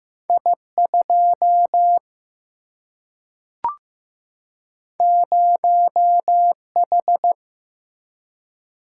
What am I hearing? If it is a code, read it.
Morse code, heard as a keyed tone, about 15 wpm, 700 Hz. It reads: I2 0H
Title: Single-Letter Single-Number 15wpm